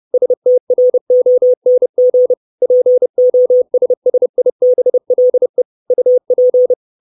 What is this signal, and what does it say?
Morse code, heard as a keyed tone, about 30 wpm, 500 Hz. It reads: STRONG POSSIBLE UP
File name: 632 Words 30WPM